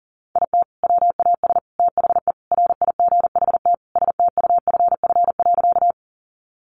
Morse code, heard as a keyed tone, about 40 wpm, 700 Hz. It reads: IT WAS THE RIGHT STUFF.
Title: Sentences from Top 1400 Words 40wpm